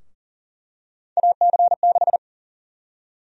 Morse code, heard as a keyed tone, about 39 wpm, 700 Hz. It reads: AC6
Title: USA Call Sign Prefixes - 40wpm